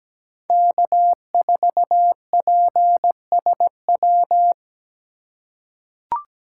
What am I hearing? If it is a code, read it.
Morse code, heard as a keyed tone, about 17 wpm, 700 Hz. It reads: K4PSW
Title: USA Call Signs - 5 Characters - 3x Character Spacing - 17wpm